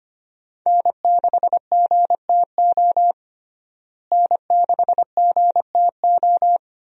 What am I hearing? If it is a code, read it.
Morse code, heard as a keyed tone, about 25 wpm, 700 Hz. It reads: N6GTO N6GTO
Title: USA Call Signs - 5 Characters - Repeated 3x - 25wpm